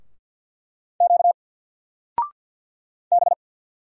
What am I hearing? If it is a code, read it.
Morse code, heard as a keyed tone, about 50 wpm, 700 Hz. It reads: Q B
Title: Single Character (Letter:Number:Punctuation:Prosign) 50wpm